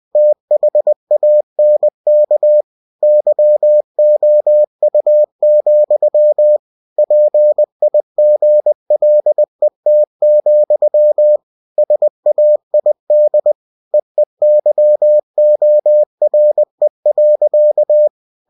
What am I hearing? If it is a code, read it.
Morse code, heard as a keyed tone, about 20 wpm, 600 Hz. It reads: THANK YOU, PIGLET, SAID EEYORE.
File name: cw-books / winnie-the-pooh / winnie-20wpm-page-0056